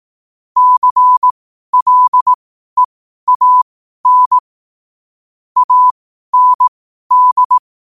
Morse code, heard as a keyed tone, about 18 wpm, spaced as for 12 wpm, 1000 Hz. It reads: CLEAN AND
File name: qotdfiles / 12wpm